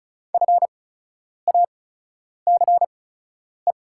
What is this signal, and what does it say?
Morse code, heard as a keyed tone, about 36 wpm, spaced as for 10 wpm, 700 Hz. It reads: FACE
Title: 8x Character Spacing - Top 300 Words - 35wpm